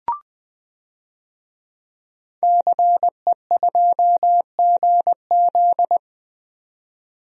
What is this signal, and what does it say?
Morse code, heard as a keyed tone, about 20 wpm, 700 Hz. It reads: CE2GZ
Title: DXCC-100 Call Signs - 5 Characters - 20wpm